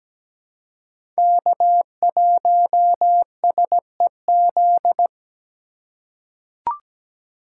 Morse code, heard as a keyed tone, about 17 wpm, 700 Hz. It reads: K1SEZ
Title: USA Call Signs - 2x Character Spacing - 17wpm